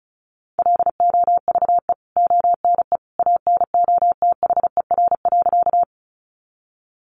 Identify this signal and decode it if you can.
Morse code, heard as a keyed tone, about 34 wpm, 700 Hz. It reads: LOVE ONE ANOTHER.
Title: Sentences from Top 600 Words 35wpm